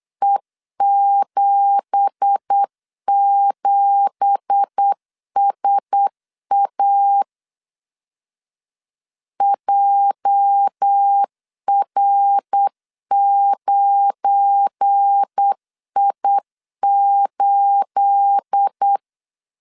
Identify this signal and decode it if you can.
Morse code, heard as a keyed tone, about 8 wpm, 800 Hz. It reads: E77SA JR9I8
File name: 8wpm